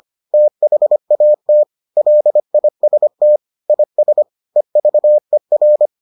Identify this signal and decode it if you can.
Morse code, heard as a keyed tone, about 25 wpm, 600 Hz. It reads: THAT LIST IS EVER